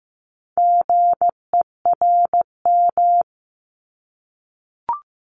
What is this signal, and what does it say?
Morse code, heard as a keyed tone, about 15 wpm, 700 Hz. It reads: GERM
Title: Multi-letter Phonemes 15wpm